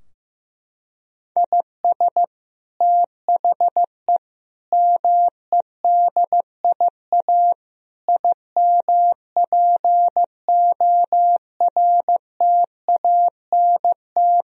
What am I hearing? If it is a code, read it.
Morse code, heard as a keyed tone, about 15 wpm, 700 Hz. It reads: IS THE MEDIA IMPORTANT
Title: Sentences from Top 1000 Words - Review Part 4 - 15wpm